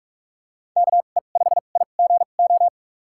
Morse code, heard as a keyed tone, about 47 wpm, spaced as for 33 wpm, 700 Hz. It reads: KE5IGO
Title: USA Call Signs - 2x Character Spacing - 45wpm